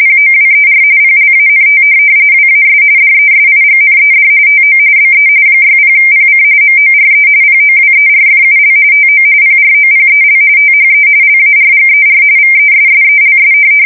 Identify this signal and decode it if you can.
RTTY, about 45.45 baud. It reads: POWER PLANT WAS DESIGNED TO BE 60PC MORE POWERFUL TO PROPEL THE LARGER VESSELS
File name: fsk170 / USS-New-Jersey